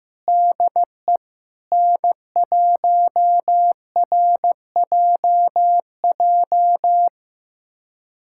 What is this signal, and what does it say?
Morse code, heard as a keyed tone, about 15 wpm, 700 Hz. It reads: DE N1RJJ
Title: USA Callsign DE Callsign - 5 Characters - 15wpm